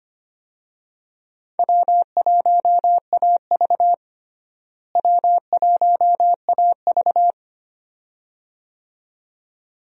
Morse code, heard as a keyed tone, about 25 wpm, 700 Hz. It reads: W1AV W1AV
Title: USA Call Signs - 4 Characters - Repeated 2x - 25wpm